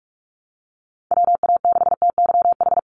Morse code, heard as a keyed tone, about 46 wpm, 700 Hz. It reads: WA6TYH
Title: USA Call Signs 45wpm